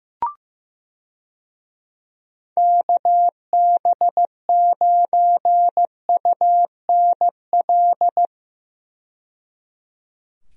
Morse code, heard as a keyed tone, about 15 wpm, 700 Hz. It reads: KB9UNL
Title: USA Call Signs - 6 Characters - 15wpm